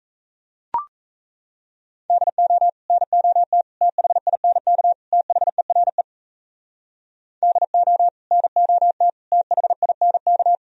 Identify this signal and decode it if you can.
Morse code, heard as a keyed tone, about 36 wpm, 700 Hz. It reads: DO NOT THINK THERE DO NOT THINK